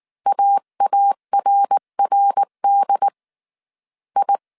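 Morse code, heard as a keyed tone, about 19 wpm, 800 Hz. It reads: AARRD I
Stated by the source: Morse code dictation at 18 wpm from the IW3SGT QRS Morse code training course, five-character groups of 300 characters.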